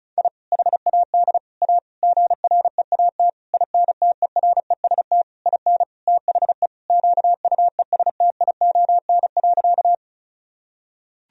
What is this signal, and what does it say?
Morse code, heard as a keyed tone, about 36 wpm, 700 Hz. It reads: I HAD A GREAT INTEREST IN THE QUESTION.
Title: Sentences from Top 1000 Words - Review Part 2 - 35wpm